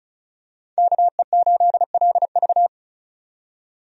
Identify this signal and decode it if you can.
Morse code, heard as a keyed tone, about 36 wpm, 700 Hz. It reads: KE8LV
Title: USA Call Signs - 5 Characters - 35wpm